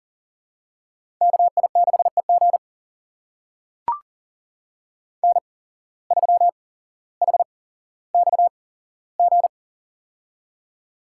Morse code, heard as a keyed tone, about 40 wpm, spaced as for 17 wpm, 700 Hz. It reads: KI6EG N3HXG
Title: USA Call Signs - 5 Characters - 8x Character Spacing - 40wpm